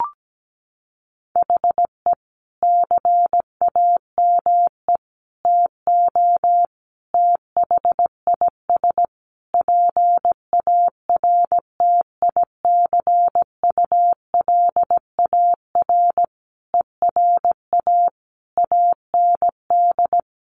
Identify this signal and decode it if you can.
Morse code, heard as a keyed tone, about 17 wpm, 700 Hz. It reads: HE CAME TO THIS PARTICULAR ERA AND
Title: A Sound Of Thunder 17wpm